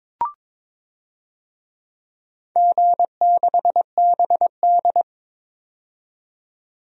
Morse code, heard as a keyed tone, about 22 wpm, 700 Hz. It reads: G6BD
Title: DXCC-100 Call Signs - 4 Characters - 22wpm